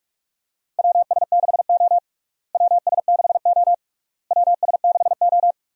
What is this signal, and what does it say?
Morse code, heard as a keyed tone, about 45 wpm, 700 Hz. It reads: WS6O WS6O WS6O
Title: USA Call Signs - 4 Characters - Repeated 3x - 45wpm